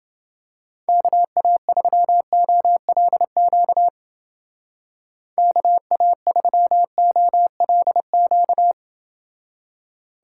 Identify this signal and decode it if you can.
Morse code, heard as a keyed tone, about 29 wpm, 700 Hz. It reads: KA3OLQ KA3OLQ